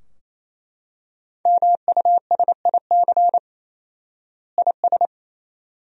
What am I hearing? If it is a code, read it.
Morse code, heard as a keyed tone, about 28 wpm, 700 Hz. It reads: MUSIC IS
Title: 4x Word-Spacing - Sentences from Top 300 Words - 28wpm